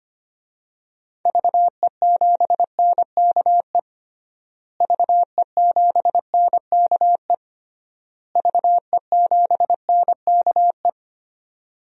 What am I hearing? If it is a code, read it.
Morse code, heard as a keyed tone, about 25 wpm, 700 Hz. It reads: VE7NKE VE7NKE VE7NKE
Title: DXCC-100 Call Signs - 6 Characters - Repeated 3x - 25wpm